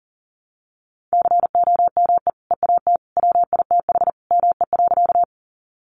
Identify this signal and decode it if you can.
Morse code, heard as a keyed tone, about 40 wpm, 700 Hz. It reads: COME EAT WITH ME.
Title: Sentences from Top 300 Words 40wpm